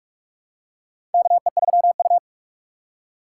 Morse code, heard as a keyed tone, about 46 wpm, 700 Hz. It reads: KE3U
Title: USA Call Signs - 4 Characters - 2x Character Spacing - 45wpm